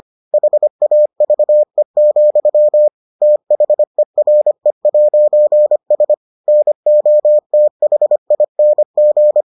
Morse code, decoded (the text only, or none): HAVE, THERE'S NOTHING